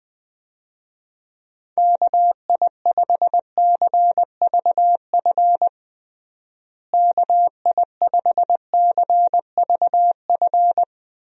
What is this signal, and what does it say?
Morse code, heard as a keyed tone, about 21 wpm, 700 Hz. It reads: KI5CVF KI5CVF
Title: USA Call Signs - 6 Characters - Repeated 2x - 20wpm